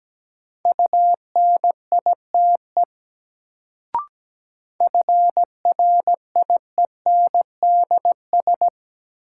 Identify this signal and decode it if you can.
Morse code, heard as a keyed tone, about 17 wpm, 700 Hz. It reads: UNITE FRIENDS
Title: Sentences from Top 1000 Words - Review Part 5 - New Word Drill - 17wpm